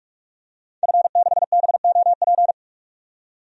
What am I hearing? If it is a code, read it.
Morse code, heard as a keyed tone, about 46 wpm, 700 Hz. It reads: F6BOP
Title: DXCC-100 Call Signs - 5 Characters - 45wpm